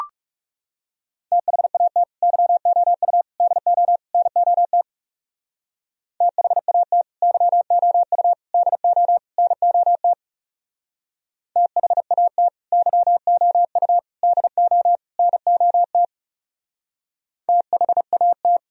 Morse code, heard as a keyed tone, about 38 wpm, 700 Hz. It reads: THAT YOU DO NOT THAT YOU DO NOT THAT YOU DO NOT THAT